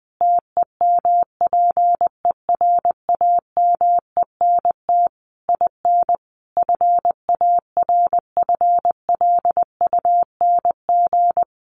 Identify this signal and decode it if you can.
Morse code, heard as a keyed tone, about 20 wpm, 700 Hz. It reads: TEMPERAMENT IN FARFLUNG
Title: A Sound Of Thunder 20wpm